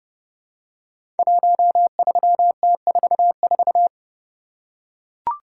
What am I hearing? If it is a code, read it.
Morse code, heard as a keyed tone, about 30 wpm, 700 Hz. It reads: 13T44
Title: Code Groups - 5 Letter:Numbers - 30wpm